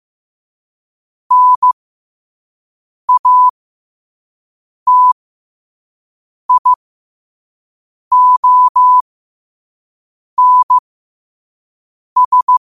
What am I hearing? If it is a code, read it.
Morse code, heard as a keyed tone, about 15 wpm, spaced as for 5 wpm, 1000 Hz. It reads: NATIONS
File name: 5wpm